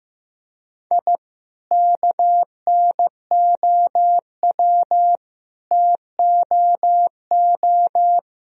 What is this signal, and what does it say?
Morse code, heard as a keyed tone, about 15 wpm, 700 Hz. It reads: I KNOW TOO
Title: Sentences from Top 1000 Words - Review Part 3 - 15wpm